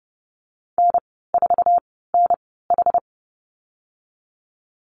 Morse code, heard as a keyed tone, about 30 wpm, spaced as for 17 wpm, 700 Hz. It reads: N4NH
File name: Lesson 15g - Number 5 - Call Signs 30wpm